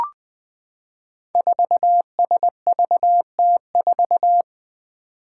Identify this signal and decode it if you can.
Morse code, heard as a keyed tone, about 20 wpm, 700 Hz. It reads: 4SVT4
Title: Five Challenging Letter-Number Combinations 20wpm